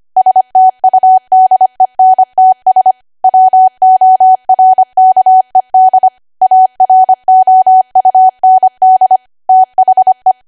Morse code, heard as a keyed tone, about 25 wpm, 750 Hz. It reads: STUDENTS WORKED AROUND THE
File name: Archive / 25 WPM / 25WPM